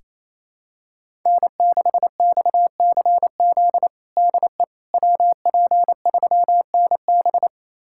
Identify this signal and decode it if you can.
Morse code, heard as a keyed tone, about 28 wpm, 700 Hz. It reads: N6XCZ DE WP3NB
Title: USA Callsign DE Callsign - 5 Characters - 28wpm